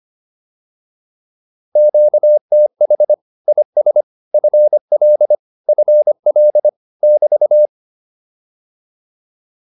Morse code, heard as a keyed tone, about 25 wpm, 600 Hz. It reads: QTH IS FL FL <BT>